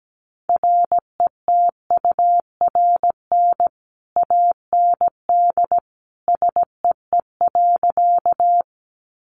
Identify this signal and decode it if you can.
Morse code, heard as a keyed tone, about 17 wpm, 700 Hz. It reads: RETURN AND SEE.
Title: Sentences from Top 800 Words 17wpm